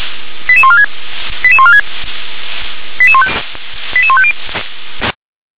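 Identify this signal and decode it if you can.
Five-tone selective call (ZVEI), heard as quick runs of five tones, tones of about 70 ms. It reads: 88146 88146 88150
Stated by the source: ZVEI five-tone selective call received with an SDR.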